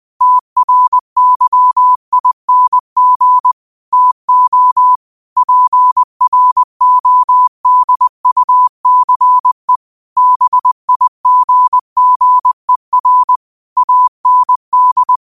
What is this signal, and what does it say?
Morse code, heard as a keyed tone, about 20 wpm, 1000 Hz. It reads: TRYING TO PRODUCE BIGGER AND
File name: qotdfiles / 20wpm